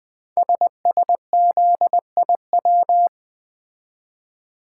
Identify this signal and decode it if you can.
Morse code, heard as a keyed tone, about 20 wpm, 700 Hz. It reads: SSZIW
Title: Code Groups - 5 Letters - 20wpm